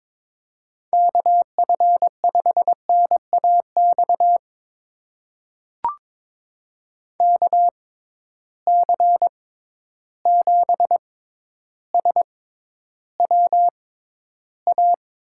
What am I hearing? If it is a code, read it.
Morse code, heard as a keyed tone, about 23 wpm, spaced as for 11 wpm, 700 Hz. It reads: KF5NAX KC7SWA
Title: USA Call Signs - 6 Characters - 6x Character Spacing - 22wpm